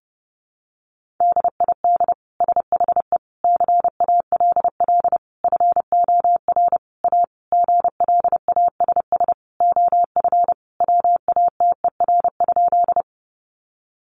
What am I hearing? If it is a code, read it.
Morse code, heard as a keyed tone, about 30 wpm, 700 Hz. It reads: DID SHE CALL FOR A GLASS OF WATER?
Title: Sentences from Top 1300 Words 30wpm